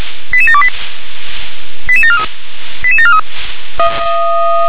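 Five-tone selective call (ZVEI), heard as quick runs of five tones, tones of about 70 ms. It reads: 88728 88743 88743
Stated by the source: ZVEI five-tone selective call received via SDR, code 88730.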